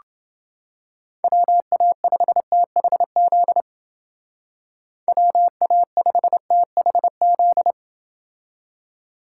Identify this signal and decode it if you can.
Morse code, heard as a keyed tone, about 28 wpm, 700 Hz. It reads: WA5THZ WA5THZ